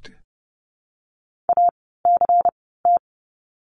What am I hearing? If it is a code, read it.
Morse code, heard as a keyed tone, about 30 wpm, 700 Hz. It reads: A C T
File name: Lesson 16f - Letter c - Words 30wpm